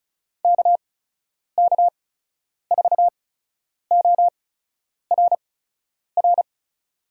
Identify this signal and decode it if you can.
Morse code, heard as a keyed tone, about 34 wpm, spaced as for 10 wpm, 700 Hz. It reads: KK4ORR
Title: USA Call Signs - 6 Characters - 8x Character Spacing - 35wpm